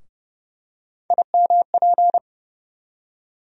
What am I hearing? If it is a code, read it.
Morse code, heard as a keyed tone, about 32 wpm, 700 Hz. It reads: IMP
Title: USA Call Sign Postfixes - 3 Letters 30wpm